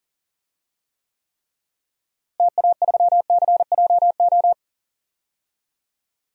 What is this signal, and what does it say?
Morse code, heard as a keyed tone, about 38 wpm, 700 Hz. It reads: TA3CJO
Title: DXCC-100 Call Signs - 6 Characters - Repeated 3x - 40wpm